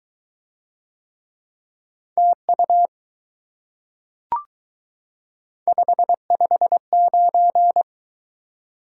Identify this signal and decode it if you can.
Morse code, heard as a keyed tone, about 20 wpm, 700 Hz. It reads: TU 559